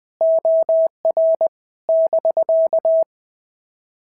Morse code, heard as a keyed tone, about 19 wpm, 650 Hz. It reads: OR <BK>
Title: POTA QSO Practice - 20wpm